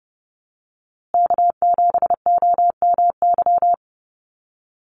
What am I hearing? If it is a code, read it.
Morse code, heard as a keyed tone, about 30 wpm, 700 Hz. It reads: K7OMY
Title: USA Call Signs Repeated 3x 30wpm